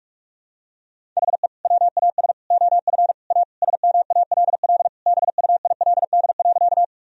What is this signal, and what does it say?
Morse code, heard as a keyed tone, about 41 wpm, 700 Hz. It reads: HE WAS OF A SMALL BUILD.
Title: Sentences from Top 1000 Words - Review Part 1 - 45wpm